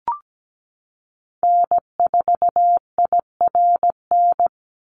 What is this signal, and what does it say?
Morse code, heard as a keyed tone, about 17 wpm, 700 Hz. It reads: N4IRN
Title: Call Signs Repeated 3x - 17wpm